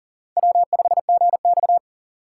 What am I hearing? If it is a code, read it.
Morse code, heard as a keyed tone, about 40 wpm, 700 Hz. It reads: W5GX
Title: USA Call Signs - 4 Characters - 4x Character Spacing - 40wpm